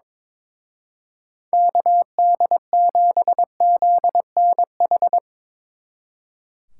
Morse code, heard as a keyed tone, about 25 wpm, 700 Hz. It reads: KD7ZNH
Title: Speed Racing - USA 2x3 Call Signs 22wpm